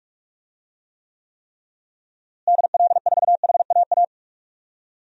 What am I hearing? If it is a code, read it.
Morse code, heard as a keyed tone, about 46 wpm, 700 Hz. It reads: DL4HAA
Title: DXCC-100 Call Signs - 6 Characters - Repeated 3x - 45wpm